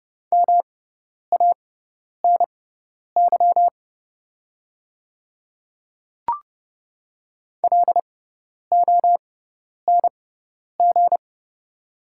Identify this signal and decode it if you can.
Morse code, heard as a keyed tone, about 31 wpm, spaced as for 10 wpm, 700 Hz. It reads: MANY LONG
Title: 6x Character Spacing - Top 100 Words - 30wpm